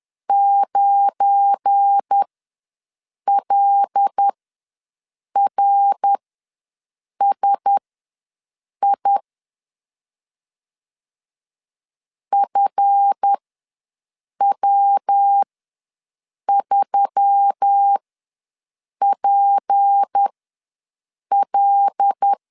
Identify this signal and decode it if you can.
Morse code, heard as a keyed tone, about 10 wpm, spaced as for 6 wpm, 800 Hz. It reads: 9LRSI FW3PL
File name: LEZIONI / 7 GF9Z / 7 GF9 ws 10wpm 4034